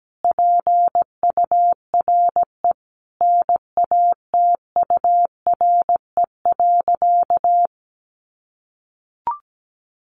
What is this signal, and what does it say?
Morse code, heard as a keyed tone, about 17 wpm, 700 Hz. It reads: PURE NATURE.
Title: Sentences from Top 1500 Words 17wpm